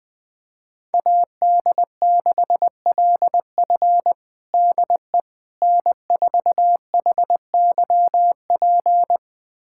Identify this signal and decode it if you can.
Morse code, heard as a keyed tone, about 20 wpm, 700 Hz. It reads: AD6LF DE N4HYP
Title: USA Callsign DE Callsign - 5 Characters - 20wpm